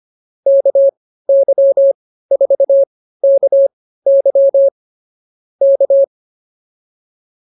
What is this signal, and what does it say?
Morse code, heard as a keyed tone, about 24 wpm, spaced as for 15 wpm, 550 Hz. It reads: KY4KY K